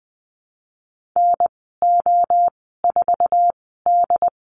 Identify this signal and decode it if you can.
Morse code, heard as a keyed tone, about 20 wpm, spaced as for 15 wpm, 700 Hz. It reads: NO4D
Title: 128 Lesson 21g - Number 6 - Call Signs 20wpm